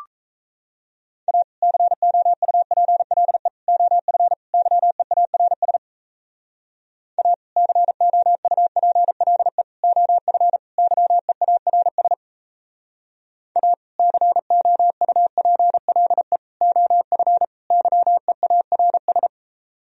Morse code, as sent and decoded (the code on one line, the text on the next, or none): .- / -.-. --- ..- .--. .-.. / --- ..-. / -.-- . .- .-. ... / .- / -.-. --- ..- .--. .-.. / --- ..-. / -.-- . .- .-. ... / .- / -.-. --- ..- .--. .-.. / --- ..-. / -.-- . .- .-. ...
A COUPLE OF YEARS A COUPLE OF YEARS A COUPLE OF YEARS